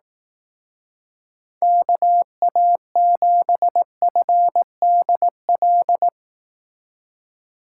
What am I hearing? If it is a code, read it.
Morse code, heard as a keyed tone, about 18 wpm, 700 Hz. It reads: KA7FDL
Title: Speed Racing - USA Call Signs 15wpm